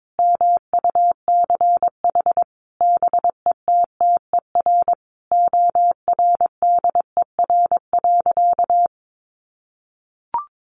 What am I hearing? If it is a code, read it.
Morse code, heard as a keyed tone, about 22 wpm, 700 Hz. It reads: MUCH BETTER ORDER.
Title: Sentences from Top 1500 Words 22wpm